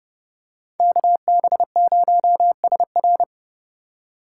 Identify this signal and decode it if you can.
Morse code, heard as a keyed tone, about 29 wpm, 700 Hz. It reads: KB0SR